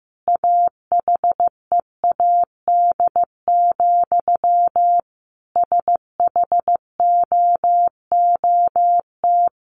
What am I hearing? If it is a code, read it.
Morse code, heard as a keyed tone, about 15 wpm, 700 Hz. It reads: AHEAD, SHOOT
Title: Sentences from Top 1200 Words 15wpm